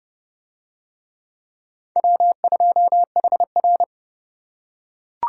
Morse code, heard as a keyed tone, about 28 wpm, 700 Hz. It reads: W2HR